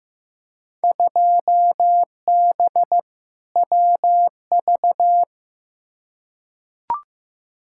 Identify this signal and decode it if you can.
Morse code, heard as a keyed tone, about 15 wpm, 700 Hz. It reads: 2B WV
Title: Field Day Exchanges Encore - 15wpm